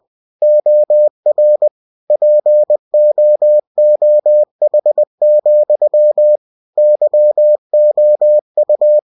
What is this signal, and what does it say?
Morse code, heard as a keyed tone, about 20 wpm, 600 Hz. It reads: OR POOH, YOU
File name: cw-books / winnie-the-pooh / winnie-20wpm-page-0034